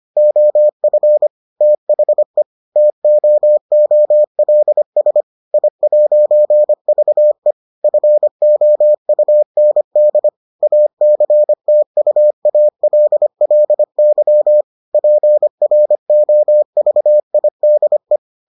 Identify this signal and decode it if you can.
Morse code, heard as a keyed tone, about 25 wpm, 600 Hz. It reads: OF THE TOOLS I'VE FOUND ACTUALLY PROVIDE